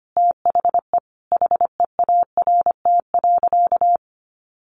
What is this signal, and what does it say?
Morse code, heard as a keyed tone, about 25 wpm, 700 Hz. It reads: THE HEART.
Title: Sentences from Top 1400 Words 25wpm